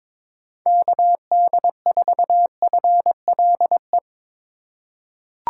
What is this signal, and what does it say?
Morse code, heard as a keyed tone, about 22 wpm, 700 Hz. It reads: KD4FLE
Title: USA Call Signs - 6 Characters - 2x Character Spacing - 22wpm